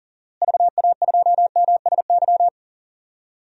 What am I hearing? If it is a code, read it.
Morse code, heard as a keyed tone, about 40 wpm, 700 Hz. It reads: VA2MSY
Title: DXCC-100 Call Signs - 6 Characters - Repeated 3x - 40wpm